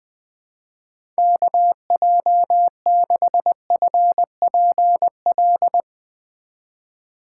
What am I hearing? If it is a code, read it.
Morse code, heard as a keyed tone, about 20 wpm, 700 Hz. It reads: KJ6FPL
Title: USA Call Signs - 6 Characters - 4x Character Spacing - 20wpm